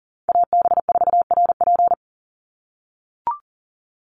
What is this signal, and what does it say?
Morse code, heard as a keyed tone, about 37 wpm, 690 Hz. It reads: AB4RP
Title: USA Call Signs 40wpm